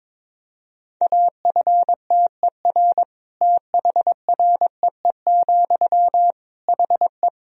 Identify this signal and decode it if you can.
Morse code, heard as a keyed tone, about 22 wpm, 700 Hz. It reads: AFTER THREE, HE
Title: Sentences from Top 200 Words and Abbreviations 22wpm